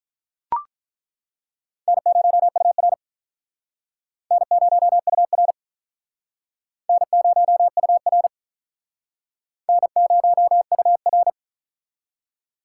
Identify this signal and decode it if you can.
Morse code, heard as a keyed tone, about 48 wpm, 700 Hz. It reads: N0UR N0UR N0UR N0UR